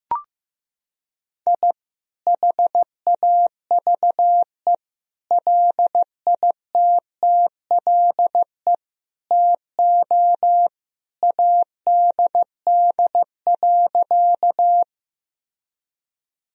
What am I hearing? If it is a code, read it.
Morse code, heard as a keyed tone, about 15 wpm, 700 Hz. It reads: I HAVE LITTLE TO ADD.
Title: Sentences from Top 1000 Words - Review Part 1 - 15wpm